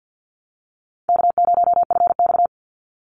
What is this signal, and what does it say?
Morse code, heard as a keyed tone, about 50 wpm, 700 Hz. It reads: K0FX
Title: Call Signs - 50wpm